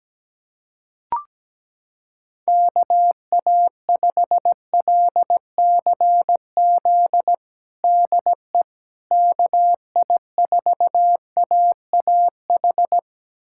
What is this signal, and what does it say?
Morse code, heard as a keyed tone, about 17 wpm, 700 Hz. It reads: KA5LCZ DE KI4AAH
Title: USA Callsign DE Callsign - 6 Characters - 17wpm